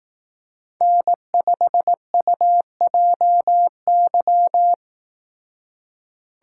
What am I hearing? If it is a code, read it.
Morse code, heard as a keyed tone, about 18 wpm, 700 Hz. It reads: N5UJY
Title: Speed Racing - USA 1x3 Call Signs 15wpm